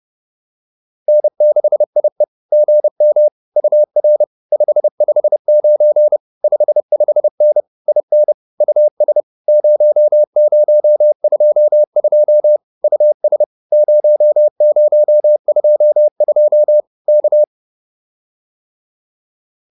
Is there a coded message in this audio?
Yes: Morse code from 1.1 to 17.4 s, heard as a keyed tone, about 30 wpm, 600 Hz. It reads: N6IE GM UR 559 55N IN US 0022 US 0022 K